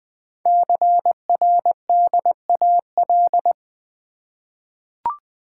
Morse code, heard as a keyed tone, about 20 wpm, 700 Hz. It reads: CRDAL
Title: Code Groups - 5 Letters - 20wpm